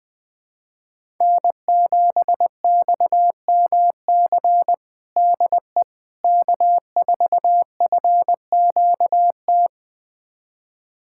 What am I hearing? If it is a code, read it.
Morse code, heard as a keyed tone, about 20 wpm, 700 Hz. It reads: N7XMC DE K4FQT
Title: USA Callsign DE Callsign 20wpm